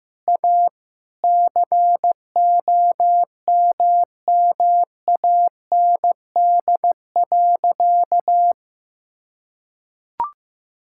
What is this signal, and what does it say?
Morse code, heard as a keyed tone, about 15 wpm, 700 Hz. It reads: A COMMAND.
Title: Sentences from Top 1600 Words 15wpm